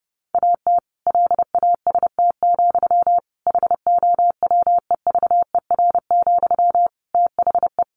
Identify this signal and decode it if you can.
Morse code, heard as a keyed tone, about 30 wpm, 700 Hz. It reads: AT LAST, HOWEVER, THE
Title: Sentences from Top 1400 Words 30wpm